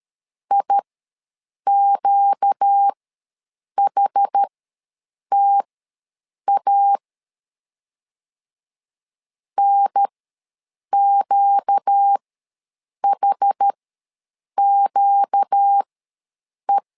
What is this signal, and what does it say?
Morse code, heard as a keyed tone, about 13 wpm, spaced as for 7 wpm, 800 Hz. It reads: IQHTA NQHQE
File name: LEZIONI / 4 Q ws 12wpm 1912